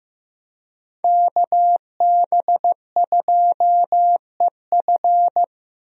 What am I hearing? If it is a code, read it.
Morse code, heard as a keyed tone, about 15 wpm, 700 Hz. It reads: KB2EF
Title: USA Call Signs - 5 Characters - Repeated 2x - 15wpm